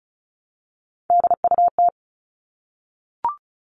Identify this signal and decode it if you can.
Morse code, heard as a keyed tone, about 34 wpm, 700 Hz. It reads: DUT
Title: All Cut Numbers - 3-Digit 35wpm